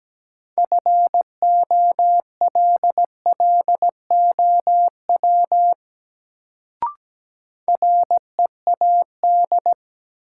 Morse code, heard as a keyed tone, about 17 wpm, 700 Hz. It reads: FOLLOW READ
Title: Sentences from Top 200 Words - New Word Drill - 17wpm